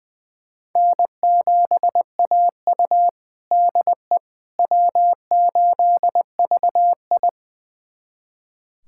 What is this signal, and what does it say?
Morse code, heard as a keyed tone, about 21 wpm, 700 Hz. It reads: N7AU DE W8VI
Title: USA Callsign DE Callsign - 4 Characters - 20wpm